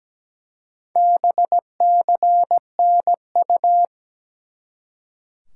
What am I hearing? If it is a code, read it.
Morse code, heard as a keyed tone, about 17 wpm, 700 Hz. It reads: BCNU
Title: Top 90 QSO Elements 17wpm